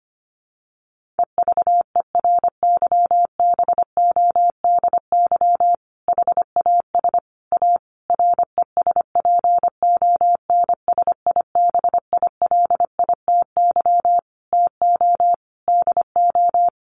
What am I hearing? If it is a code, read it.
Morse code, heard as a keyed tone, about 25 wpm, 700 Hz. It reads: EVERYBODY HAS A RESPONSIBILITY TO DO
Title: Sentences from Top 1400 Words 25wpm